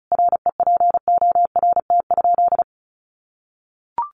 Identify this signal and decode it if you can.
Morse code, heard as a keyed tone, about 34 wpm, 700 Hz. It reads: REPORT?
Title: Sentences from Top 1300 Words 35wpm